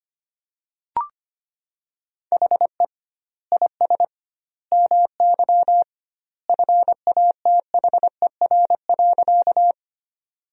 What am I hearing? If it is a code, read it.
Morse code, heard as a keyed tone, about 25 wpm, 700 Hz. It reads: HE IS MY FATHER.
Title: 2x Word-Spacing - Sentences from Top 300 Words - 25wpm